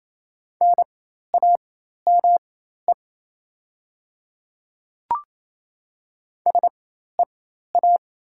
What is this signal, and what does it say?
Morse code, heard as a keyed tone, about 28 wpm, spaced as for 13 wpm, 700 Hz. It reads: NAME SEA
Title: 4x Character Spacing - Top 300 Words - 28wpm